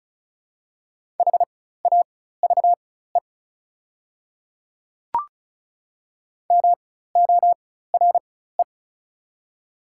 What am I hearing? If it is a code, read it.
Morse code, heard as a keyed tone, about 36 wpm, spaced as for 16 wpm, 690 Hz. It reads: HAVE MORE
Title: 4x Character Spacing - Top 100 Words - 35wpm